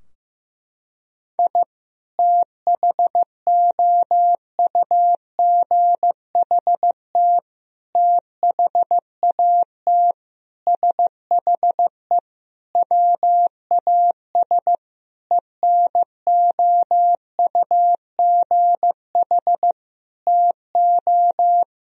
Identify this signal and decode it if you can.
Morse code, heard as a keyed tone, about 16 wpm, 700 Hz. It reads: I THOUGHT THAT SHE WAS ENOUGH TO
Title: Sentences from Top 1000 Words - Review Part 3 - 15wpm